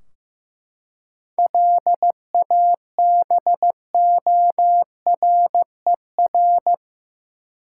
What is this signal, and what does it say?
Morse code, heard as a keyed tone, about 16 wpm, 700 Hz. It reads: LABORER
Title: Practice QSO - Exchange 3 - More Info - 15wpm